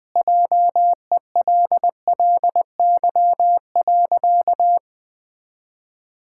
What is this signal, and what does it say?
Morse code, heard as a keyed tone, about 20 wpm, 700 Hz. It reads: JELLY.
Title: Pangram Challenge v2 - 20wpm